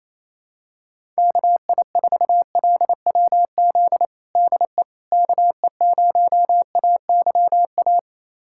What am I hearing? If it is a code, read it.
Morse code, heard as a keyed tone, about 28 wpm, 700 Hz. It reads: KI4LWZ DE KE0AYA
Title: USA Callsign DE Callsign - 6 Characters - 28wpm